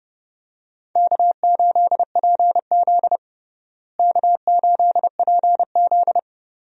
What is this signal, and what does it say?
Morse code, heard as a keyed tone, about 32 wpm, 700 Hz. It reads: K8PZ K8PZ
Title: USA Call Signs - 4 Characters - Repeated 3x - 30wpm